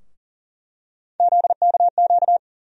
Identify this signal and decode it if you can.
Morse code, heard as a keyed tone, about 40 wpm, 700 Hz. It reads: ZKQ
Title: USA Call Sign Postfixes - 3 Letters 40wpm